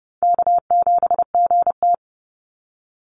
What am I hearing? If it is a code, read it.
Morse code, heard as a keyed tone, about 30 wpm, 700 Hz. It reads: K7GT
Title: NA SOTA Chaser Callsigns 30wpm